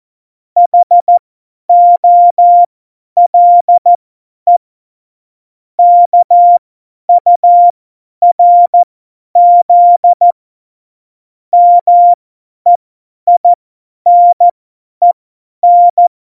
Morse code, heard as a keyed tone, about 14 wpm, spaced as for 10 wpm, 700 Hz. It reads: HOLE KURZ MEINEN
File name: Morsekurs / Witze / Witz 30. Oktober 2018, 50 F70